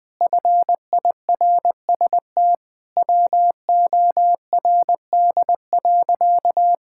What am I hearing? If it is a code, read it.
Morse code, heard as a keyed tone, about 20 wpm, 700 Hz. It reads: FIRST WORD.
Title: Pangram Challenge v2 - 20wpm